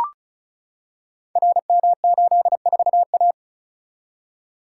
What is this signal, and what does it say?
Morse code, heard as a keyed tone, about 34 wpm, 700 Hz. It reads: RM84A
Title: Code Groups - 5 Letter:Numbers - 35wpm